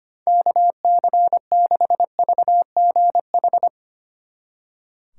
Morse code, heard as a keyed tone, about 33 wpm, 700 Hz. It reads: KC6VGH